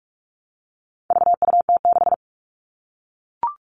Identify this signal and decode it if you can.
Morse code, heard as a keyed tone, about 44 wpm, 700 Hz. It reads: VUT6
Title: All Cut Numbers - 4-Digit 45wpm